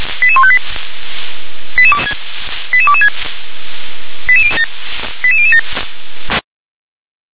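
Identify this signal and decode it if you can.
Five-tone selective call (ZVEI), heard as quick runs of five tones, tones of about 70 ms. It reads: 88158 88226 88007
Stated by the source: ZVEI five-tone selective call received with an SDR.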